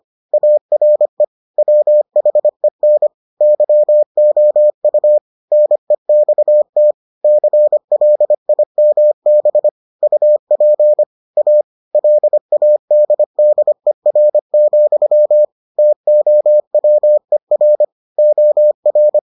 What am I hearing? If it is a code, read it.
Morse code, heard as a keyed tone, about 25 wpm, 600 Hz. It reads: ARE WHEN YOU NEXT CLIMB UP A LADDER, TOWER OR